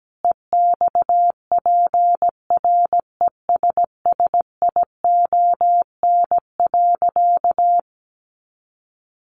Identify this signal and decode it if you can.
Morse code, heard as a keyed tone, about 17 wpm, 700 Hz. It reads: EXPRESSION.
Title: Sentences from Top 1200 Words 17wpm